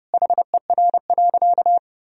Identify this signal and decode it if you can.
Morse code, heard as a keyed tone, about 32 wpm, 700 Hz. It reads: HER.
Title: Sentences from Top 1000 Words - Review Part 1 - 30wpm